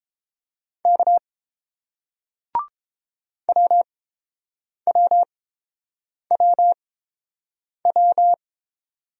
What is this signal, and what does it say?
Morse code, heard as a keyed tone, about 30 wpm, spaced as for 7 wpm, 700 Hz. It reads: K WWWW